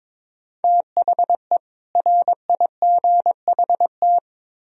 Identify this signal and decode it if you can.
Morse code, heard as a keyed tone, about 24 wpm, 700 Hz. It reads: THE RIGHT
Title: Speed-Racing - Sets of 2 Words 22wpm